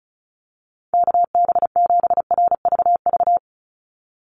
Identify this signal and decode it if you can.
Morse code, heard as a keyed tone, about 34 wpm, 700 Hz. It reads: KB7RVV
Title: USA Call Signs 35wpm